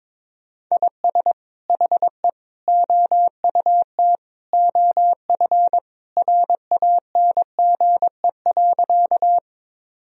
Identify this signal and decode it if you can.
Morse code, heard as a keyed tone, about 22 wpm, 700 Hz. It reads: IS HE OUT OF RANGE.
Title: Sentences from Top 1000 Words - Review Part 4 - 22wpm